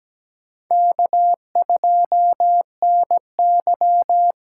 Morse code, heard as a keyed tone, about 20 wpm, 700 Hz. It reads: K2NY
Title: Speed Racing - USA 1x2 Call Signs 17wpm